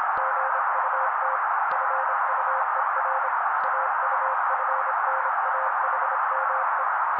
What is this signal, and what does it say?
Morse code, heard as a keyed tone, about 26 wpm, 530 Hz. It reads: GUT FUER AUFNAHME
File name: daily / 25wpm noise